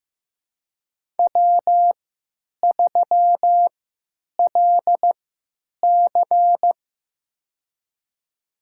Morse code, heard as a keyed tone, about 15 wpm, spaced as for 9 wpm, 700 Hz. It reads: W3LC
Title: USA Call Signs - 4 Characters - 3x Character Spacing - 15wpm